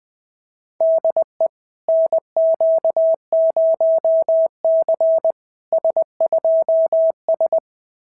Morse code, heard as a keyed tone, about 20 wpm, 660 Hz. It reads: DE NQ0C S2S
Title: SOTA QSO Practice 20wpm